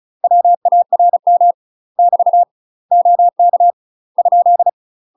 Morse code, heard as a keyed tone, about 34 wpm, 700 Hz. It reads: WARM <BT> OK ?
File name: UK MASTER QSOs TRAINING / 35WPM / UK MASTER QSOs 0008